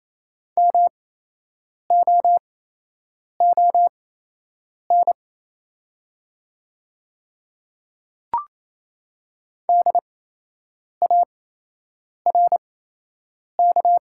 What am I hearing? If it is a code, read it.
Morse code, heard as a keyed tone, about 28 wpm, spaced as for 8 wpm, 700 Hz. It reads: MOON DARK
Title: 8x Character Spacing - Top 500 Words - 28wpm